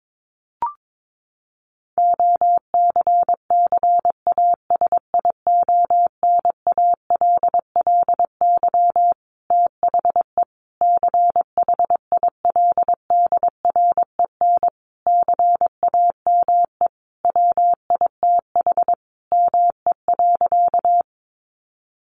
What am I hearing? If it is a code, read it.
Morse code, heard as a keyed tone, about 22 wpm, 700 Hz. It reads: OCCASIONALLY THE CHILDREN CAME WITH ME.
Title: Sentences from Top 1300 Words 22wpm